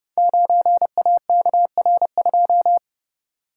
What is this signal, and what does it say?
Morse code, heard as a keyed tone, about 30 wpm, 700 Hz. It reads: 9AKR2
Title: Code Groups - 5 Letter:Numbers - 30wpm